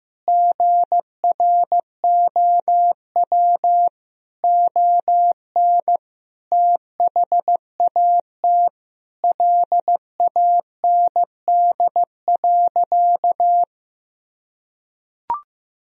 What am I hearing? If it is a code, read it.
Morse code, heard as a keyed tone, about 15 wpm, 700 Hz. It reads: GROW ON THAT LAND.
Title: Sentences from Top 1000 Words - Review Part 1 - 15wpm